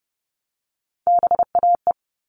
Morse code, heard as a keyed tone, about 30 wpm, 700 Hz. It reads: BAE
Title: All Cut Numbers - 3-Digit 30wpm